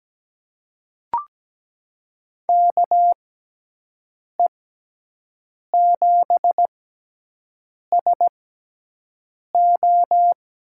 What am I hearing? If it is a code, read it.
Morse code, heard as a keyed tone, about 17 wpm, spaced as for 6 wpm, 700 Hz. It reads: KE7SO
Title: USA Call Signs - 6x Character Spacing - 17wpm